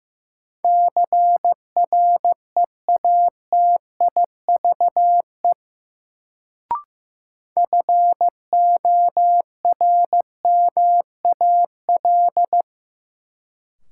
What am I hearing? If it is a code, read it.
Morse code, heard as a keyed tone, about 16 wpm, 700 Hz. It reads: CREATIVE FORMAL
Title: Sentences from Top 1300 Words - New Word Drill Alt - 15wpm